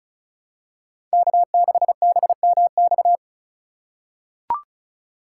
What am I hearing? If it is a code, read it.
Morse code, heard as a keyed tone, about 36 wpm, 700 Hz. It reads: K6BMX
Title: USA Call Signs - 2x Character Spacing - 35wpm